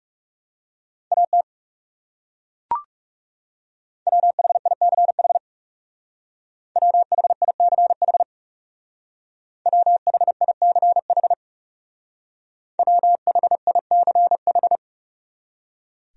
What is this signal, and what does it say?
Morse code, heard as a keyed tone, about 38 wpm, 700 Hz. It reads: AT WHICH WHICH WHICH WHICH
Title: Speed-Racing Top 300 Words 30wpm